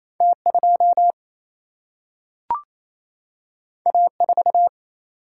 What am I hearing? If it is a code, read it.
Morse code, heard as a keyed tone, about 28 wpm, 700 Hz. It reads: T 2 A 4
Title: Single-Letter Single-Number 28wpm